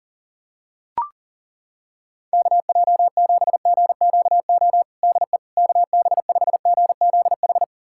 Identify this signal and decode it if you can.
Morse code, heard as a keyed tone, about 40 wpm, 700 Hz. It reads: KJ7GQO DE KB5GZH
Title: USA Callsign DE Callsign - 6 Characters - 40wpm